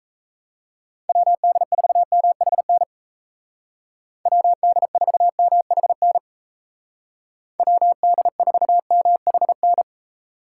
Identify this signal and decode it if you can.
Morse code, heard as a keyed tone, about 35 wpm, 700 Hz. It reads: WD4MHN WD4MHN WD4MHN
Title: Speed Racing - USA 2x3 Call Signs 28wpm